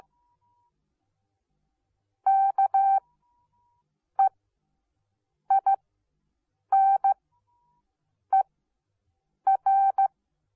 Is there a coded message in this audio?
Yes: Morse code from 2.3 to 10.1 s, heard as a keyed tone, about 15 wpm, spaced as for 6 wpm, 780 Hz. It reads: KEINER